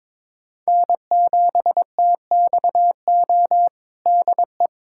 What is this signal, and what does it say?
Morse code, heard as a keyed tone, about 22 wpm, 700 Hz. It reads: N7TXO DE
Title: USA Callsign DE Callsign - 5 Characters - 22wpm